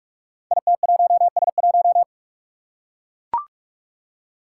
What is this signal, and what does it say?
Morse code, heard as a keyed tone, about 44 wpm, 700 Hz. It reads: IT1S1
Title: Five Challenging Letter-Number Combinations 45wpm